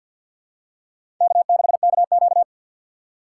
Morse code, heard as a keyed tone, about 52 wpm, 700 Hz. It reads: K6KQ
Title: USA Call Signs - 4 Characters - 3x Character Spacing - 50wpm